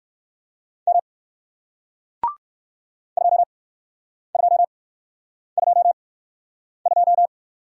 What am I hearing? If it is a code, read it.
Morse code, heard as a keyed tone, about 57 wpm, spaced as for 9 wpm, 700 Hz. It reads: M 2222